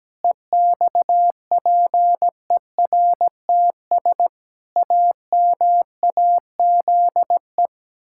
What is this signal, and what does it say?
Morse code, heard as a keyed tone, about 17 wpm, 700 Hz. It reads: EXPERTS AMAZE
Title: Pangram Challenge v2 - 17wpm